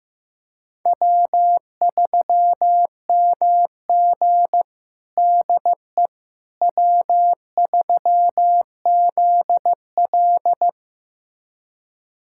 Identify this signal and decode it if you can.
Morse code, heard as a keyed tone, about 15 wpm, 700 Hz. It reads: W3MG DE W3ZL
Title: USA Callsign DE Callsign - 4 Characters - 15wpm